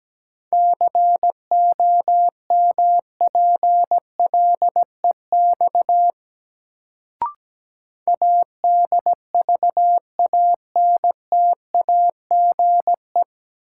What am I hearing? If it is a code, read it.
Morse code, heard as a keyed tone, about 17 wpm, 700 Hz. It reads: COMPLEX ADVANTAGE
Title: Sentences from Top 900 Words - New Word Drill - 17wpm